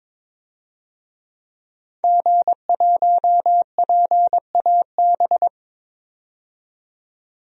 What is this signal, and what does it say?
Morse code, heard as a keyed tone, about 22 wpm, 700 Hz. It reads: G1PAB
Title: DXCC-100 Call Signs - 5 Characters - 22wpm